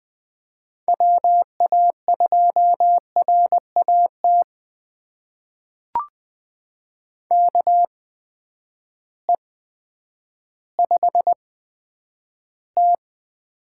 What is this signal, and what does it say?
Morse code, heard as a keyed tone, about 20 wpm, spaced as for 10 wpm, 700 Hz. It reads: WA2RAT KE5T
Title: USA Call Signs - 6 Characters - 8x Character Spacing - 20wpm